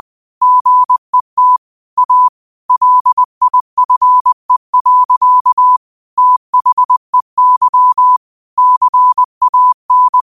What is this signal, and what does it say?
Morse code, heard as a keyed tone, about 20 wpm, 1000 Hz. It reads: GET A LIFE. THEY CAN
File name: qotdfiles / 20wpm